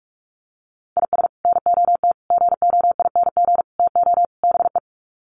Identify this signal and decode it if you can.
Morse code, heard as a keyed tone, about 44 wpm, 700 Hz. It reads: IS NOT GOING TO BE
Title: Sets of 5 Words 45wpm